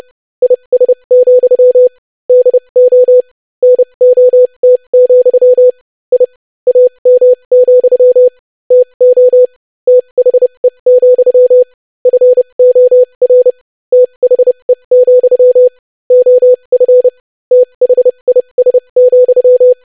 Morse code, heard as a keyed tone, about 30 wpm, spaced as for 25 wpm, 500 Hz. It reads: IS, DO NOT, I AM, TO THE, FOR THE, OF THIS,
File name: lesson / basicphrase-25wpm